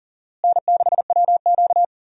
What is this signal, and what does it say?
Morse code, heard as a keyed tone, about 39 wpm, 700 Hz. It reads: N6WQ
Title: USA Call Signs - 4 Characters - Repeated 3x - 40wpm